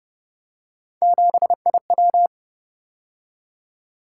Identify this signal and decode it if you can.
Morse code, heard as a keyed tone, about 30 wpm, 700 Hz. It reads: 7IW